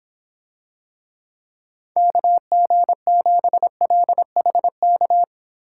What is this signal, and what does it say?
Morse code, heard as a keyed tone, about 25 wpm, 700 Hz. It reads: KG7LHK